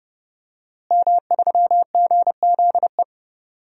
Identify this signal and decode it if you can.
Morse code, heard as a keyed tone, about 30 wpm, 700 Hz. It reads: M3GZE
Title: DXCC-100 Call Signs - 5 Characters - Repeated 2x - 30wpm